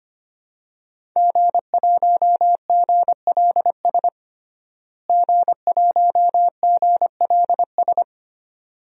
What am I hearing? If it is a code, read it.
Morse code, heard as a keyed tone, about 25 wpm, 700 Hz. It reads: G1GLS G1GLS
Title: DXCC-100 Call Signs - 5 Characters - Repeated 2x - 25wpm